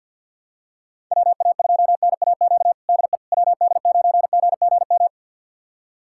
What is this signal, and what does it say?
Morse code, heard as a keyed tone, about 53 wpm, 700 Hz. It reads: WA2NAQ DE WD9GGM
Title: USA Callsign DE Callsign - 6 Characters - 50wpm